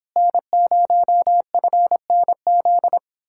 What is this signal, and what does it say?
Morse code, heard as a keyed tone, about 29 wpm, 700 Hz. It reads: N0FNZ